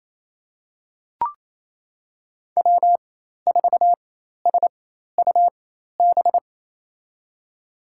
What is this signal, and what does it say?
Morse code, heard as a keyed tone, about 28 wpm, spaced as for 13 wpm, 700 Hz. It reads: W4SUB
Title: USA Call Signs - 4x Character Spacing - 28wpm